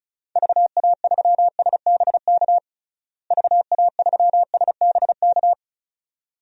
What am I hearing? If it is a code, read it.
Morse code, heard as a keyed tone, about 36 wpm, 700 Hz. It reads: VA3SBK VA3SBK
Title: DXCC-100 Call Signs - 6 Characters - Repeated 2x - 35wpm